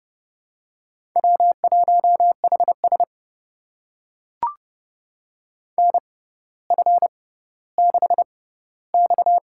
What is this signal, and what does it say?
Morse code, heard as a keyed tone, about 30 wpm, spaced as for 15 wpm, 700 Hz. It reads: W1HS NF6X
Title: USA Call Signs - 4 Characters - 6x Character Spacing - 30wpm